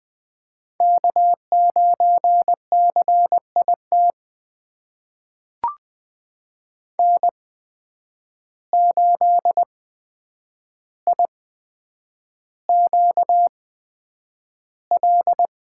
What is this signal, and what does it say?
Morse code, heard as a keyed tone, about 20 wpm, spaced as for 9 wpm, 700 Hz. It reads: K9CIT N8IQL
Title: USA Call Signs - 5 Characters - 8x Character Spacing - 20wpm